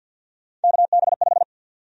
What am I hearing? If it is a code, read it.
Morse code, heard as a keyed tone, about 53 wpm, 700 Hz. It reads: KB5
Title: USA Call Sign Prefixes - 50wpm